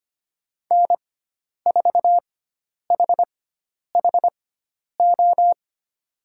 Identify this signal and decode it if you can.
Morse code, heard as a keyed tone, about 25 wpm, spaced as for 10 wpm, 700 Hz. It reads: N4HHO
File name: Lesson 16g - Letter c - Call Signs 25wpm